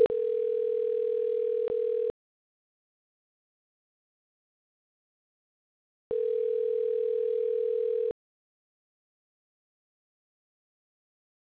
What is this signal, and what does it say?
Ringback tone (North America), 440 + 480 Hz.